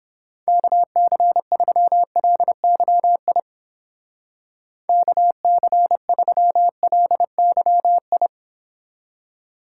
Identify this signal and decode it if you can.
Morse code, heard as a keyed tone, about 28 wpm, 700 Hz. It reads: KC3LYI KC3LYI